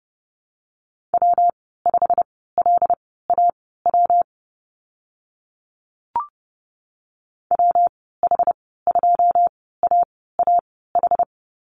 Morse code, heard as a keyed tone, about 30 wpm, spaced as for 17 wpm, 700 Hz. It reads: W5LAW WH2AAH
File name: Lesson 19g - Letter w - Call Signs 30wpm